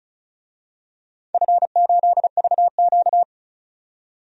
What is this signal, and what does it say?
Morse code, heard as a keyed tone, about 36 wpm, 700 Hz. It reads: F8VQ
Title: DXCC-100 Call Signs - 4 Characters - 35wpm